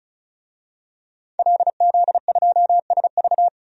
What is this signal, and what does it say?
Morse code, heard as a keyed tone, about 34 wpm, 700 Hz. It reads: LZ2SV
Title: DXCC-100 Call Signs - 5 Characters - 35wpm